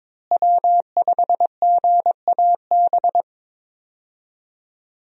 Morse code, heard as a keyed tone, about 22 wpm, 700 Hz. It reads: W5GAB
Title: USA Call Signs - 5 Characters - Repeated 2x - 22wpm